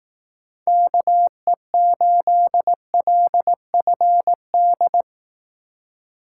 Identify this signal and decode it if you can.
Morse code, heard as a keyed tone, about 18 wpm, 700 Hz. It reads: KE8LFD
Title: Speed Racing - USA Call Signs 15wpm